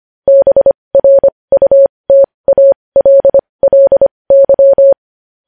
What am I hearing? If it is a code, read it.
Morse code, heard as a keyed tone, about 25 wpm, spaced as for 20 wpm, 550 Hz. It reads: BRUTALLY
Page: news headlines in morse code at 20 wpm